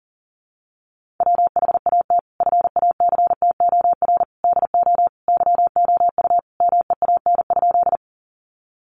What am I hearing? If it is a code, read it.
Morse code, heard as a keyed tone, about 40 wpm, 700 Hz. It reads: WHAT FACTOR DO YOU MEAN?
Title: Sentences from Top 1300 Words 40wpm